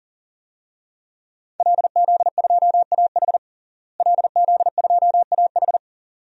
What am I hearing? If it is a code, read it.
Morse code, heard as a keyed tone, about 40 wpm, 700 Hz. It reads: LZ2AH LZ2AH
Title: DXCC-100 Call Signs - 5 Characters - Repeated 2x - 40wpm